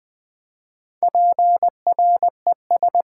Morse code, heard as a keyed tone, about 20 wpm, 700 Hz. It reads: PRES
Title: Build A Word - 4 Letters At A Time - 12-Letter Words - 20wpm